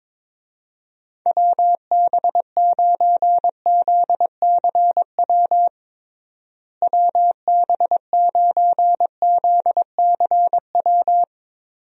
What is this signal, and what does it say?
Morse code, heard as a keyed tone, about 22 wpm, 700 Hz. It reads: WB9ZCW WB9ZCW
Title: USA Call Signs - 6 Characters - Repeated 2x - 22wpm